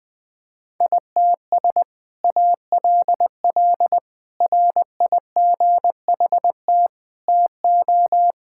Morse code, heard as a keyed tone, about 21 wpm, 700 Hz. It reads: ITS ALL RIGHT TO
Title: Sentences with 2-Word Phrases - Spoken Beginning - 20wpm